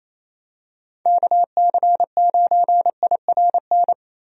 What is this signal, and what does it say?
Morse code, heard as a keyed tone, about 28 wpm, 700 Hz. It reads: KC9IRN
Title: USA Call Signs - 6 Characters - Repeated 3x - 28wpm